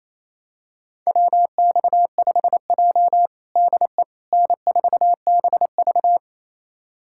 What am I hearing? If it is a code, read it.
Morse code, heard as a keyed tone, about 28 wpm, 700 Hz. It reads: WX5J DE N4BV
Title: USA Callsign DE Callsign - 4 Characters - 28wpm